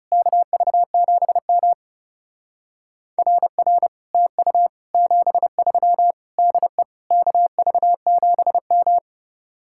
Morse code, heard as a keyed tone, about 38 wpm, 700 Hz. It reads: KV7M RR TU 73 DE KV7M